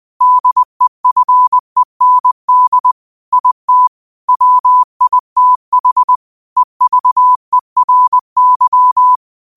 Morse code, heard as a keyed tone, about 20 wpm, 1000 Hz. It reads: DEFEND IT WITH EVERY